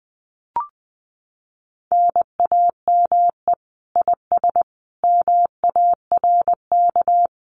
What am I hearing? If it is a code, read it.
Morse code, heard as a keyed tone, about 20 wpm, 700 Hz. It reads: NAME IS MARK
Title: Common QSO Phrases Encore 20wpm